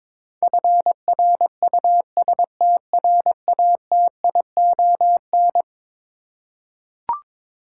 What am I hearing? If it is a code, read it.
Morse code, heard as a keyed tone, about 22 wpm, 700 Hz. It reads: FRUSTRATION
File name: practice / 22wpm / Eleven Letter Words - 22wpm